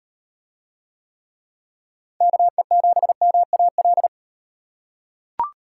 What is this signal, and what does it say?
Morse code, heard as a keyed tone, about 34 wpm, 700 Hz. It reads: KE7MAL